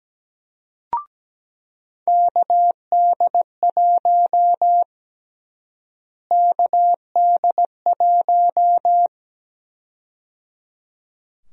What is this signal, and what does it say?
Morse code, heard as a keyed tone, about 17 wpm, 700 Hz. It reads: KD1 KD1
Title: USA Call Sign Prefixes - Repeated 2x - 17wpm